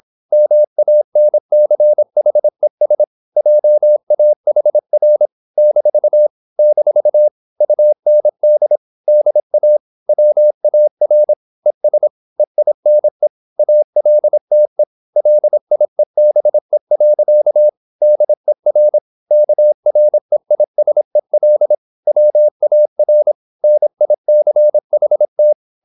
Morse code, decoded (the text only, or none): MANCHES JAHR - - UND DA WAR ES EINE ALTE LIEBE. DER KREISEL WAR NICHT